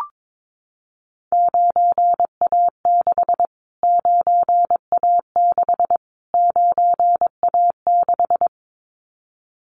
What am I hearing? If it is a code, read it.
Morse code, heard as a keyed tone, about 22 wpm, 700 Hz. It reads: 9A6 9A6 9A6
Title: Callsign Prefixes for Top 25 DXCC - Repeated 3x - 22wpm